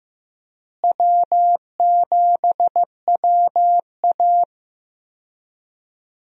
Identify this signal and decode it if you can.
Morse code, heard as a keyed tone, about 16 wpm, 700 Hz. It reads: W7WA
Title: USA Call Signs - 4 Characters - Repeated 3x - 15wpm